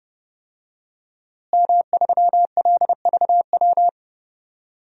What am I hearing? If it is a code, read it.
Morse code, heard as a keyed tone, about 29 wpm, 700 Hz. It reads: M3LVW
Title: DXCC-100 Call Signs - 5 Characters - Repeated 3x - 30wpm